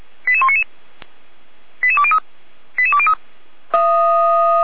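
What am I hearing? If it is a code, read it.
Five-tone selective call (ZVEI), heard as quick runs of five tones, tones of about 70 ms. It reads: 88199 88203 88203
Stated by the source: ZVEI five-tone selective call received by SDR.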